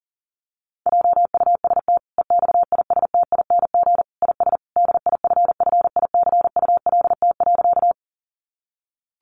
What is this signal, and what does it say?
Morse code, heard as a keyed tone, about 37 wpm, 700 Hz. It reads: JUST EXISTING IS DIFFICULT.
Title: Sentences from Top 1500 Words 40wpm